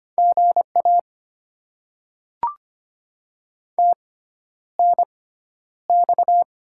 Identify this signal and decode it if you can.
Morse code, heard as a keyed tone, about 25 wpm, spaced as for 11 wpm, 690 Hz. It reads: GA TNX
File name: Top 20 QSO Elements - 6x Character Spacing - 25wpm